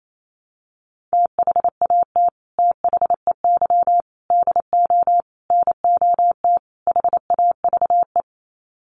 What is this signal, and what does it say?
Morse code, heard as a keyed tone, about 28 wpm, 700 Hz. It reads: THAT THEY DO NOT HAVE
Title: Sets of 5 Words Encore2 28wpm